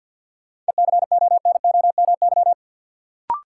Morse code, heard as a keyed tone, about 53 wpm, 700 Hz. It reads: ECONOMY
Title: Sentences from Top 1000 Words - Review Part 3 - New Word Drill - 50wpm